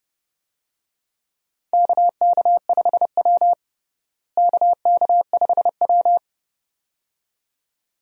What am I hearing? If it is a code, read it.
Morse code, heard as a keyed tone, about 30 wpm, 700 Hz. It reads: KK5W KK5W
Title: USA Call Signs - 4 Characters - Repeated 2x - 30wpm